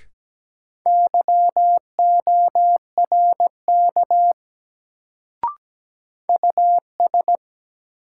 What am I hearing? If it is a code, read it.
Morse code, heard as a keyed tone, about 17 wpm, 700 Hz. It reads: YORK US
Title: Sets of 2 Words - New Word Drill - 17wpm